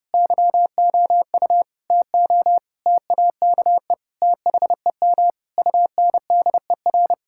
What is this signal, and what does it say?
Morse code, heard as a keyed tone, about 30 wpm, 700 Hz. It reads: YOU TO TAKE THEM UNDER
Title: Sentences from Top 1600 Words 30wpm